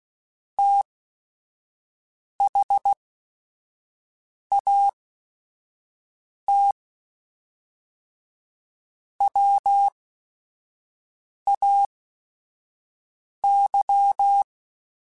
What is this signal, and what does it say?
Morse code, heard as a keyed tone, about 16 wpm, spaced as for 5 wpm, 800 Hz. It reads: THAT WAY
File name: QSOs / QSOs-mp3s / WA7QOX 1@5